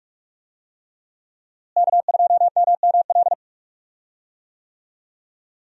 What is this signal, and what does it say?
Morse code, heard as a keyed tone, about 41 wpm, 700 Hz. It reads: K2MML
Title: USA Call Signs - 5 Characters - Repeated 3x - 45wpm